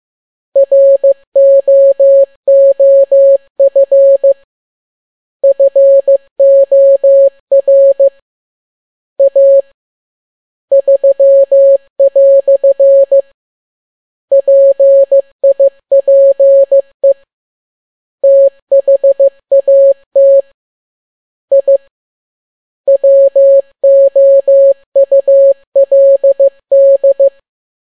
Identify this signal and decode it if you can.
Morse code, heard as a keyed tone, about 15 wpm, 550 Hz. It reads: ROOF FOR A 3" PIPE THAT I WOULD
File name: n6tt p1 15C13